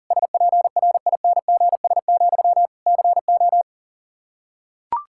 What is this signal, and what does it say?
Morse code, heard as a keyed tone, about 40 wpm, 700 Hz. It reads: SPRINGS, CO
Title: Practice QSO - Exchange 1 - Beginning - 40wpm